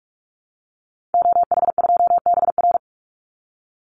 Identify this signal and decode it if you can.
Morse code, heard as a keyed tone, about 44 wpm, 700 Hz. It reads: OH2BR
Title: Call Signs - 45wpm